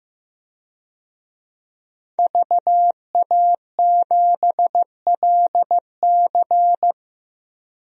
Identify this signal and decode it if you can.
Morse code, heard as a keyed tone, about 15 wpm, 700 Hz. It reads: VA7LC
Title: DXCC-100 Call Signs - 5 Characters - 15wpm